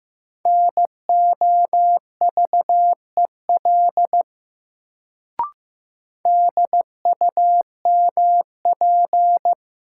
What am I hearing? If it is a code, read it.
Morse code, heard as a keyed tone, about 15 wpm, 700 Hz. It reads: NOVEL DUMP
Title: Sentences from Top 1300 Words - New Word Drill - 15wpm